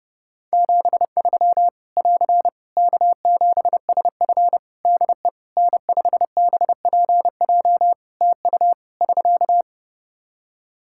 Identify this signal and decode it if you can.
Morse code, heard as a keyed tone, about 30 wpm, 700 Hz. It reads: 73 <AR> K7SF DE N5BPJ TU <SK>
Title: Practice QSO - Exchange 4 - Ending - 30wpm